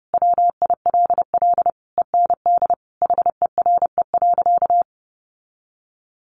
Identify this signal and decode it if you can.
Morse code, heard as a keyed tone, about 30 wpm, 700 Hz. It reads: WILL END HERE.
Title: Sentences from Top 1200 Words 30wpm